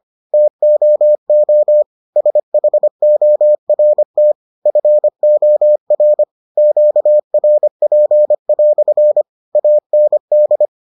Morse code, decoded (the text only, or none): TOO SHORT FOR QRP" AND